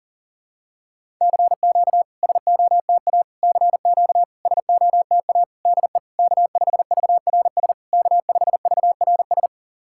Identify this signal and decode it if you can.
Morse code, heard as a keyed tone, about 40 wpm, 690 Hz. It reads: CQ SOTA CQ SOTA DE K5VRS K5VRS
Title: SOTA QSO Practice 40wpm